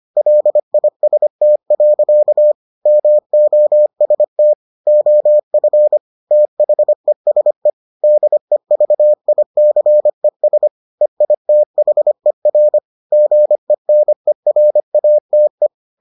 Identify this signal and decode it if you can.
Morse code, heard as a keyed tone, about 25 wpm, 600 Hz. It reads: LIST. MOST OF THESE DEVICES EITHER GENERATE